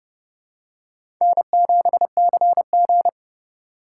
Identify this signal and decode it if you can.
Morse code, heard as a keyed tone, about 30 wpm, 700 Hz. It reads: N7CG
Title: USA Call Signs - 4 Characters - 3x Character Spacing - 30wpm